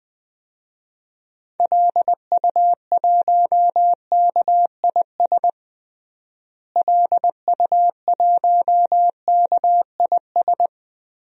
Morse code, heard as a keyed tone, about 20 wpm, 700 Hz. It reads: LU1KIS LU1KIS
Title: DXCC-100 Call Signs - 6 Characters - Repeated 2x - 20wpm